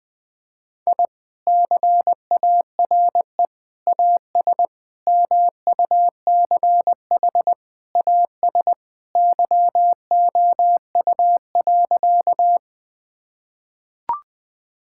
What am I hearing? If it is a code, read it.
Morse code, heard as a keyed tone, about 20 wpm, 700 Hz. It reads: I CARE AS MUCH AS YOU.
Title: Sentences with 3-Word Phrases - 20wpm